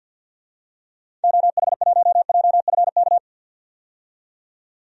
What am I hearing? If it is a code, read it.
Morse code, heard as a keyed tone, about 53 wpm, 700 Hz. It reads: OH1JFK
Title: DXCC-100 Call Signs - 6 Characters - 50wpm